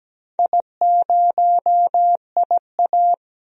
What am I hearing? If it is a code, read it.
Morse code, heard as a keyed tone, about 17 wpm, 700 Hz. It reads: I0IA
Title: DXCC-100 Call Signs - 4 Characters - 17wpm